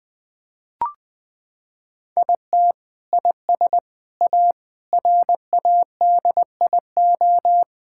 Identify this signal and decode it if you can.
Morse code, heard as a keyed tone, about 20 wpm, 700 Hz. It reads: IT IS A RADIO
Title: Sentences from Top 1000 Words - Review Part 5 - 20wpm